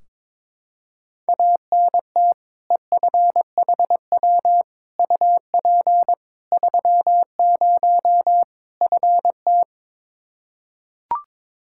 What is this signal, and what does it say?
Morse code, heard as a keyed tone, about 22 wpm, 700 Hz. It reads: ANT EFHW UP 30 FT
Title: Practice QSO - Exchange 2 - Info - 22wpm